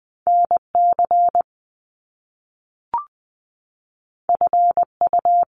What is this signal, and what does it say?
Morse code, heard as a keyed tone, about 20 wpm, 700 Hz. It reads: N C F U
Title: Top Two Letter Combinations in Words 20wpm